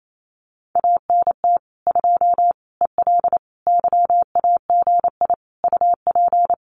Morse code, heard as a keyed tone, about 28 wpm, 700 Hz. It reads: ANT 2 EL YAGI UP
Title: Common QSO Phrases Encore 28wpm